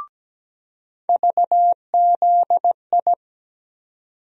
Morse code, heard as a keyed tone, about 17 wpm, 700 Hz. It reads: VZI
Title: USA Call Sign Postfixes - 3 Letters 17wpm